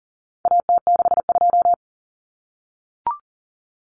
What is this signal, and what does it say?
Morse code, heard as a keyed tone, about 40 wpm, 700 Hz. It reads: AT62
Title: Common Cut Numbers - 4-Digit 40wpm